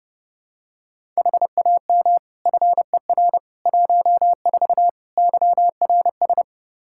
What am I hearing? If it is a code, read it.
Morse code, heard as a keyed tone, about 30 wpm, 700 Hz. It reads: HAM FER 14 YRS
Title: K1USN SST QSO Practice 30wpm